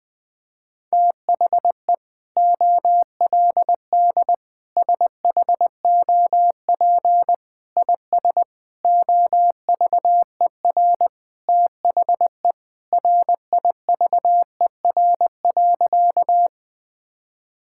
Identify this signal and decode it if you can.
Morse code, heard as a keyed tone, about 21 wpm, 700 Hz. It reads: THE OLD SHOP IS OVER THE RIVER.
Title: Sentences from Top 1600 Words 20wpm